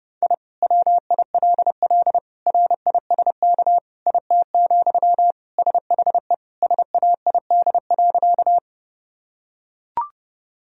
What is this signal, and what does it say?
Morse code, heard as a keyed tone, about 32 wpm, 700 Hz. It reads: I WILL RISK IT, SHE SAID.
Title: Sentences from Top 1000 Words - Review Part 4 - 30wpm